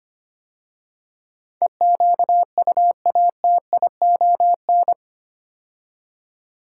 Morse code, heard as a keyed tone, about 25 wpm, 690 Hz. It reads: EQUATION
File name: 8 Letter Words - First 3 Letters Given - 25wpm